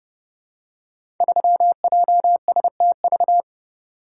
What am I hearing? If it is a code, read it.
Morse code, heard as a keyed tone, about 30 wpm, 700 Hz. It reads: 3JSTV
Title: Code Groups - 5 Letter:Numbers - 30wpm